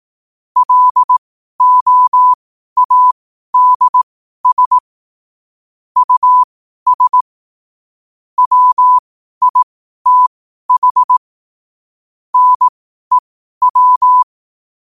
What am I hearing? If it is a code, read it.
Morse code, heard as a keyed tone, about 18 wpm, spaced as for 12 wpm, 1000 Hz. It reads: LOADS US WITH NEW
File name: qotdfiles / 12wpm